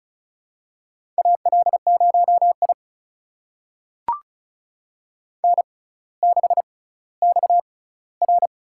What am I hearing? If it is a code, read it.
Morse code, heard as a keyed tone, about 36 wpm, spaced as for 18 wpm, 700 Hz. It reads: AL0I N6XR
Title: USA Call Signs - 4 Characters - 6x Character Spacing - 35wpm